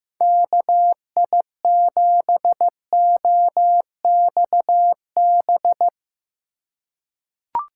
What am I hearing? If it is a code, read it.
Morse code, heard as a keyed tone, about 15 wpm, 700 Hz. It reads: KI7OXB
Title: USA Call Signs - 4x Character Spacing - 15wpm